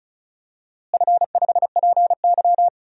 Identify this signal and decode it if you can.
Morse code, heard as a keyed tone, about 34 wpm, 700 Hz. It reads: F5PY
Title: DXCC-100 Call Signs - 4 Characters - Repeated 3x - 35wpm